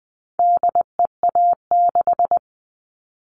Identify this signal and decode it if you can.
Morse code, heard as a keyed tone, about 20 wpm, 700 Hz. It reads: DEA6
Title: All Cut Numbers - 4-Digit 20wpm